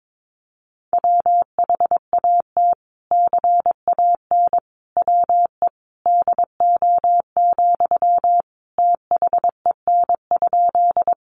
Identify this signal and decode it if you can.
Morse code, heard as a keyed tone, about 22 wpm, 700 Hz. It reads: WHAT CAN WE DO, THEN?
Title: Sentences from Top 100 Words 22wpm